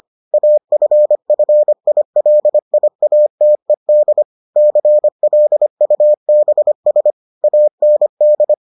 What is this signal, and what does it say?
Morse code, heard as a keyed tone, about 25 wpm, 600 Hz. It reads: AFFILIATED CLUBS AND